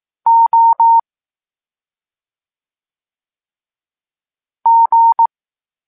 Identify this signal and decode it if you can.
Morse code, heard as a keyed tone, about 18 wpm, 920 Hz. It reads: O G
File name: Tekst-23 4